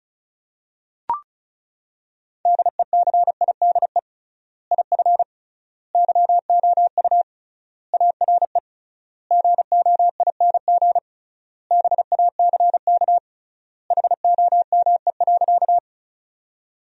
Morse code, heard as a keyed tone, about 36 wpm, 700 Hz. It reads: DECIDE IF YOU ARE GOING BACK HOME.
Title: 3x Word-Spacing - Sentences from Top 500 Words - 35wpm